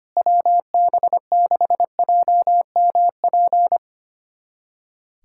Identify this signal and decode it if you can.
Morse code, heard as a keyed tone, about 25 wpm, 700 Hz. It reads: WB6JMP
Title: USA Callsign DE Callsign 25wpm